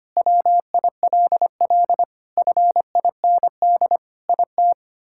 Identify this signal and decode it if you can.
Morse code, heard as a keyed tone, about 25 wpm, 700 Hz. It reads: WILL FIND IT
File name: practice / 25wpm / Sentences with 3-Word Phrases - 25wpm